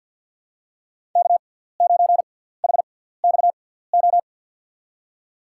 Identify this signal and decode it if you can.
Morse code, heard as a keyed tone, about 48 wpm, spaced as for 17 wpm, 700 Hz. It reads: K9HXO
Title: USA Call Signs - 5 Characters - 6x Character Spacing - 50wpm